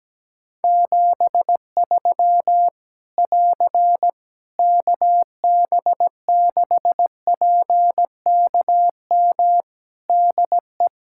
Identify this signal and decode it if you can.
Morse code, heard as a keyed tone, about 17 wpm, 700 Hz. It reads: 73 <AR> KB6PKM DE
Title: Practice QSO - Exchange 4 - Ending - 17wpm